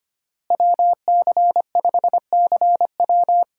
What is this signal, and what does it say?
Morse code, heard as a keyed tone, about 25 wpm, 700 Hz. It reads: WC5CW
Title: USA Call Signs - 5 Characters - Repeated 3x - 25wpm